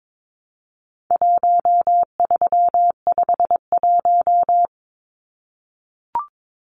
Morse code, heard as a keyed tone, about 22 wpm, 700 Hz. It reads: 1351
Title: Numbers - 4-Digit 22wpm